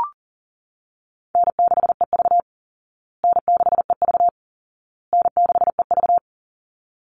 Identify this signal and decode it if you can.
Morse code, heard as a keyed tone, about 40 wpm, 700 Hz. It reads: N6EV N6EV N6EV
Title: Easy Call Signs Repeated 3x - 40wpm